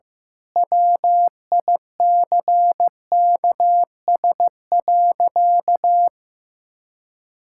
Morse code, heard as a keyed tone, about 15 wpm, 700 Hz. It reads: WICKS.
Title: Pangram Challenge v2 - 15wpm